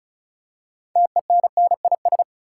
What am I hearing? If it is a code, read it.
Morse code, heard as a keyed tone, about 36 wpm, 700 Hz. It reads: TENNIS
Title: Six Letter Words - 35wpm